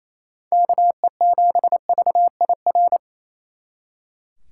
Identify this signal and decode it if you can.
Morse code, heard as a keyed tone, about 35 wpm, 700 Hz. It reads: KE7VIR
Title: Speed Racing - USA 2x3 Call Signs 28wpm